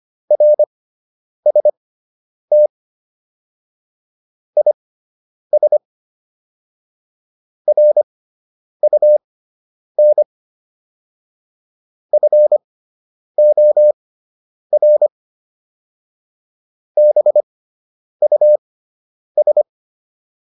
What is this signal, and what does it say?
Morse code, heard as a keyed tone, about 25 wpm, spaced as for 9 wpm, 600 Hz. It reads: RST IS RUN FOR BUS